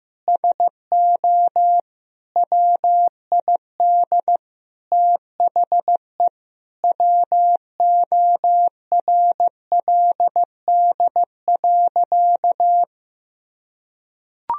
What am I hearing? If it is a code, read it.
Morse code, heard as a keyed tone, about 15 wpm, 700 Hz. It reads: SO WID THE WORLD.
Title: Sentences from Top 200 Words and Abbreviations 15wpm